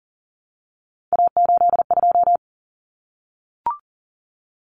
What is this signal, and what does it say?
Morse code, heard as a keyed tone, about 40 wpm, 700 Hz. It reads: A82
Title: Common Cut Numbers - 3-Digit 40wpm